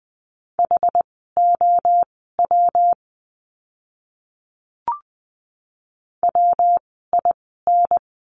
Morse code, heard as a keyed tone, about 20 wpm, spaced as for 15 wpm, 700 Hz. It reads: HOW WIN
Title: 115 Lesson 19f - Letter w - Words 20wpm